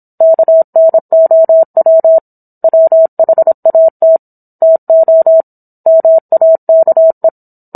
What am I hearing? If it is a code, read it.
Morse code, heard as a keyed tone, about 27 wpm, 650 Hz. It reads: KNOW WHAT TO MAKE
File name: Book One / 17-Chapter 17 - 26 WPM